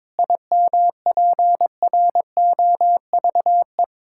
Morse code, heard as a keyed tone, about 22 wpm, 700 Hz. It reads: IMPROVE
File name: Difficult Sentences with Hints - 22wpm